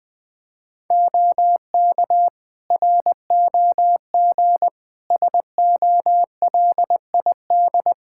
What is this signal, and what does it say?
Morse code, heard as a keyed tone, about 20 wpm, 700 Hz. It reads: OK ROG SOLID
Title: Practice QSO - Exchange 3 - More Info - 20wpm